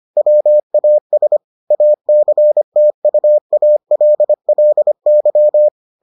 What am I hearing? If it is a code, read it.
Morse code, heard as a keyed tone, about 25 wpm, 600 Hz. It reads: WAS ACTUALLY